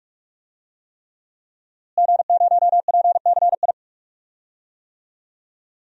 Morse code, heard as a keyed tone, about 44 wpm, 700 Hz. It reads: G0PCI
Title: DXCC-100 Call Signs - 5 Characters - Repeated 3x - 45wpm